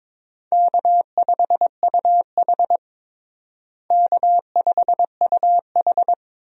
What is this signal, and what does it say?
Morse code, heard as a keyed tone, about 22 wpm, 700 Hz. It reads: K5UH K5UH
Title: USA Call Signs - 4 Characters - Repeated 2x - 22wpm